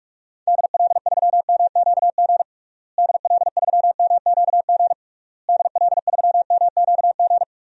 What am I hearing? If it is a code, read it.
Morse code, heard as a keyed tone, about 46 wpm, 700 Hz. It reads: DL3MQG DL3MQG DL3MQG
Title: DXCC-100 Call Signs - 6 Characters - Repeated 3x - 45wpm